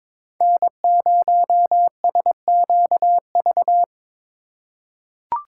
Morse code, heard as a keyed tone, about 21 wpm, 700 Hz. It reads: N0SQV
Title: USA Call Signs - 4x Character Spacing - 22wpm